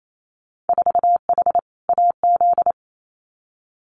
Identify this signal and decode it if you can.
Morse code, heard as a keyed tone, about 28 wpm, 700 Hz. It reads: 4H AZ
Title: Winter Field Day Exchanges Encore 28wpm